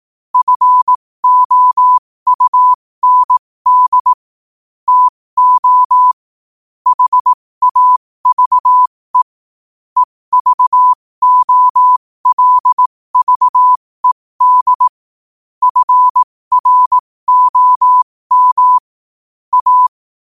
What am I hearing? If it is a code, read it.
Morse code, heard as a keyed tone, about 18 wpm, spaced as for 15 wpm, 1000 Hz. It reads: FOUND TO HAVE EVOLVED FROM A
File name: qotdfiles / 15wpm